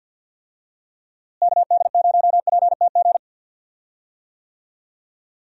Morse code, heard as a keyed tone, about 52 wpm, 700 Hz. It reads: KD0PTG
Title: USA Call Signs - 6 Characters - Repeated 3x - 50wpm